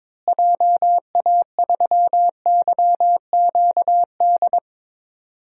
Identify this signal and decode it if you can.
Morse code, heard as a keyed tone, about 21 wpm, 700 Hz. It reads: JA3YQD
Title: DXCC-100 Call Signs - 6 Characters - Repeated 3x - 22wpm